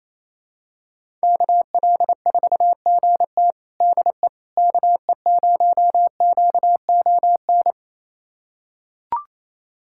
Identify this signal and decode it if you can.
Morse code, heard as a keyed tone, about 28 wpm, 700 Hz. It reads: KL4GT DE KE0QON
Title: USA Callsign DE Callsign 28wpm